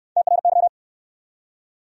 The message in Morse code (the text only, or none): TRY